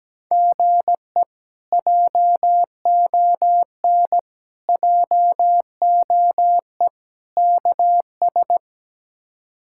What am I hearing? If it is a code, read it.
Morse code, heard as a keyed tone, about 17 wpm, 700 Hz. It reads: GE JON JOE KS
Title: K1USN SST QSO Practice - 17wpm